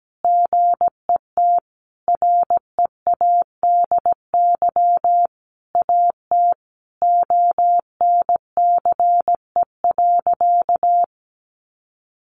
Morse code, heard as a keyed tone, about 17 wpm, 700 Hz. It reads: GET READY AT ONCE.
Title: Sentences from Top 1100 Words 17wpm